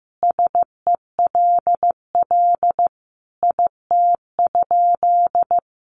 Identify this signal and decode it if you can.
Morse code, heard as a keyed tone, about 15 wpm, 700 Hz. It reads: SELL IT?
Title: Sentences from Top 1100 Words 15wpm